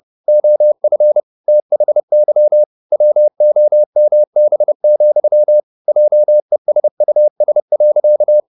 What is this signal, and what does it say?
Morse code, heard as a keyed tone, about 30 wpm, 600 Hz. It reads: OF THY WOMB, JESUS.
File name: the old man and the sea / the old man and the sea 1 30wpm